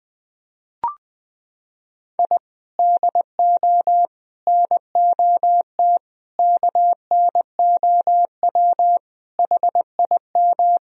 Morse code, heard as a keyed tone, about 20 wpm, 700 Hz. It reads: I DO NOT KNOW HIM
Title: Sentences from Top 1000 Words - Review Part 1 - 20wpm